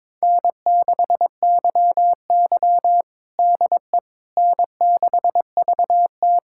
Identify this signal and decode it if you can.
Morse code, heard as a keyed tone, about 22 wpm, 700 Hz. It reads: N6YY DE N6VT
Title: USA Callsign DE Callsign - 4 Characters - 22wpm